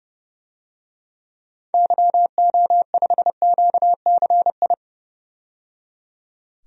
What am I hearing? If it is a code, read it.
Morse code, heard as a keyed tone, about 30 wpm, 700 Hz. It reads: YO5QCI
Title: DXCC-100 Call Signs - 6 Characters - 30wpm